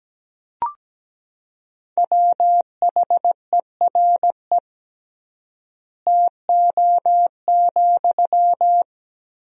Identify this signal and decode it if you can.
Morse code, heard as a keyed tone, about 17 wpm, 700 Hz. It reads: WHERE TO,
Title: 3x Word-Spacing - Sentences from Top 200 Words - 17wpm